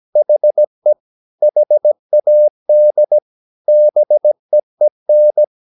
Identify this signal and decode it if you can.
Morse code, heard as a keyed tone, about 17 wpm, 600 Hz. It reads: HE HAD BEEN
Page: Romanen Never af Ken Follet på engelsk.